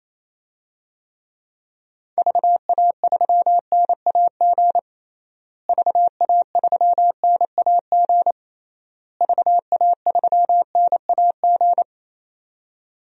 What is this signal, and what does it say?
Morse code, heard as a keyed tone, about 28 wpm, 700 Hz. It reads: VA3NAG VA3NAG VA3NAG
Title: DXCC-100 Call Signs - 6 Characters - Repeated 3x - 28wpm